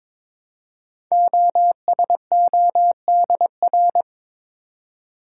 Morse code, heard as a keyed tone, about 22 wpm, 700 Hz. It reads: OSODR
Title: Code Groups - 5 Letters - 22wpm